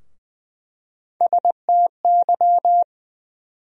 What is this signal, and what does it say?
Morse code, heard as a keyed tone, about 20 wpm, 700 Hz. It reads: STY
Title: USA Call Sign Postfixes - 3 Letters 20wpm